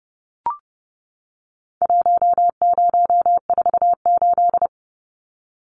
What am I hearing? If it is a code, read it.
Morse code, heard as a keyed tone, about 30 wpm, 700 Hz. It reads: 1048
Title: Numbers - Contest Style - 4-Digit 30wpm